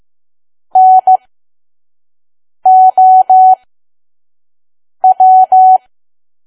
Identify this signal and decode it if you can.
Morse code, heard as a keyed tone, about 15 wpm, spaced as for 5 wpm, 750 Hz. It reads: NOW
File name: Archive / 5 WPM files / PM